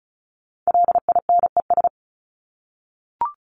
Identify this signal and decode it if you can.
Morse code, heard as a keyed tone, about 34 wpm, 700 Hz. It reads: LINES
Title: ES:S Suffix - Top 100 Words with Base Words 35wpm